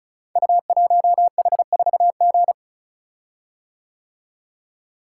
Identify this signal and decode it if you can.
Morse code, heard as a keyed tone, about 36 wpm, 690 Hz. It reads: U1H4G
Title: Code Groups - 5 Letter:Numbers - 35wpm